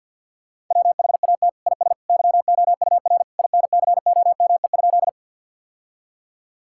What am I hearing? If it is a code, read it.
Morse code, heard as a keyed tone, about 50 wpm, 700 Hz. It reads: WHAT IS YOUR INCOME?
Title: Sentences from Top 1000 Words - Review Part 5 - 50wpm